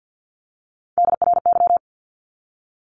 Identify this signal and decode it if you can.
Morse code, heard as a keyed tone, about 50 wpm, 700 Hz. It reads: DRY
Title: Top 500 Words 50wpm